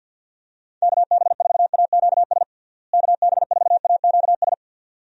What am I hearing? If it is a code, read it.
Morse code, heard as a keyed tone, about 52 wpm, 700 Hz. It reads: KB4AQS KB4AQS
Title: USA Call Signs - 6 Characters - Repeated 2x - 50wpm